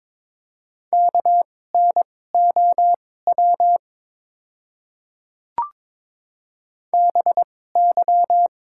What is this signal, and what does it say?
Morse code, heard as a keyed tone, about 22 wpm, spaced as for 16 wpm, 700 Hz. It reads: KNOW BY
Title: 2x Character Spacing - Top 500 Words - 22wpm